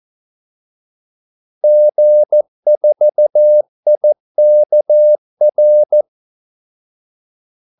Morse code, heard as a keyed tone, about 14 wpm, 600 Hz. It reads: G4IKR
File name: Callsigns-14wpm